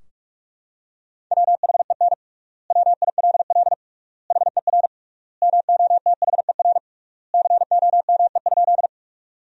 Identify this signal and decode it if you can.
Morse code, heard as a keyed tone, about 46 wpm, 700 Hz. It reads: WHEN WILL HER MOTHER COME?
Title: 3x Word-Spacing - Sentences from Top 200 Words - 45wpm